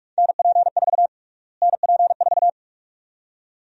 Morse code, heard as a keyed tone, about 41 wpm, 700 Hz. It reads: NP4 NP4
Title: USA Call Sign Prefixes - Repeated 2x - 45wpm